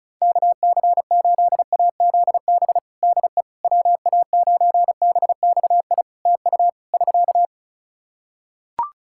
Morse code, heard as a keyed tone, about 36 wpm, 700 Hz. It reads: KC8AZB DE WA9BXI TU <SK>
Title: K1USN SST QSO Practice 35wpm